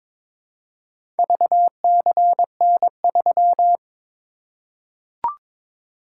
Morse code, heard as a keyed tone, about 22 wpm, 700 Hz. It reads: VCN3
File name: practice / 22wpm / Four Letter-Number Combinations - 22wpm